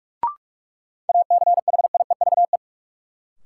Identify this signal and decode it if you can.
Morse code, heard as a keyed tone, about 45 wpm, 700 Hz. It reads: ACHIEVE
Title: Sentences from Top 1000 Words - Review Part 5 - New Word Drill - 45wpm